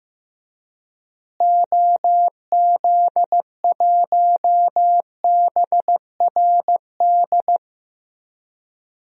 Morse code, heard as a keyed tone, about 15 wpm, 700 Hz. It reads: OZ1BRD
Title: DXCC-100 Call Signs - 6 Characters - Repeated 3x - 15wpm